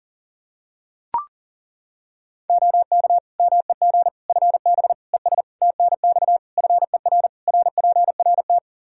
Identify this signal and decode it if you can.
Morse code, heard as a keyed tone, about 40 wpm, 700 Hz. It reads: OK MEG FB ES TNX FER RPRT
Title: Practice QSO - Exchange 2 - Info - 40wpm